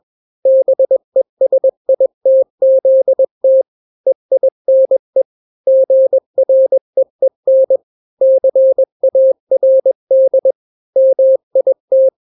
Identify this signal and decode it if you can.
Morse code, heard as a keyed tone, about 21 wpm, 530 Hz. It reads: BESITZT EINE GREEN CARD MIT
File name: daily / 20wpm